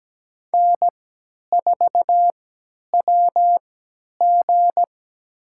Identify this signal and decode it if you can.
Morse code, heard as a keyed tone, about 17 wpm, spaced as for 10 wpm, 700 Hz. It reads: N4WG
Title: USA Call Signs - 4 Characters - 3x Character Spacing - 17wpm